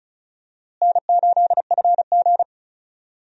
Morse code, heard as a keyed tone, about 34 wpm, 700 Hz. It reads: N8FG
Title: USA Call Signs - 4 Characters - Repeated 2x - 35wpm